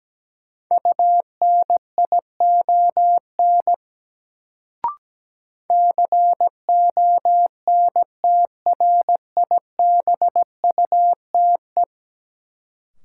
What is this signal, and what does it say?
Morse code, heard as a keyed tone, about 17 wpm, 700 Hz. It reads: UNION CONTRIBUTE
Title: Sentences from Top 1400 Words - New Word Drill - 17wpm